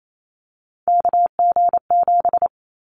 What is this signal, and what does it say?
Morse code, heard as a keyed tone, about 28 wpm, 700 Hz. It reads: KG7
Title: Callsign Prefixes for Top 25 DXCC - Repeated 3x - 28wpm